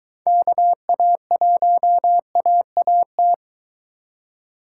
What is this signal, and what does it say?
Morse code, heard as a keyed tone, about 23 wpm, 700 Hz. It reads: KA1AAT